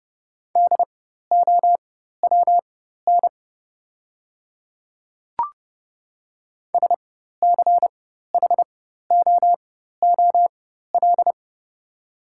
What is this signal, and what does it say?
Morse code, heard as a keyed tone, about 30 wpm, spaced as for 14 wpm, 700 Hz. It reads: DOWN SCHOOL
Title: 4x Character Spacing - Top 300 Words - 30wpm